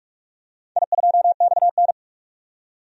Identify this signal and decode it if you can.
Morse code, heard as a keyed tone, about 44 wpm, 690 Hz. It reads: I2XN
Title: DXCC-100 Call Signs - 4 Characters - 45wpm